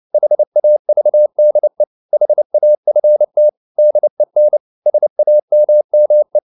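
Morse code, heard as a keyed tone, about 29 wpm, 600 Hz. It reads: HAVDE HAFT DEN SAMME
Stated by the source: Fast (QRQ) Morse at 29 wpm, from CW med Gnister.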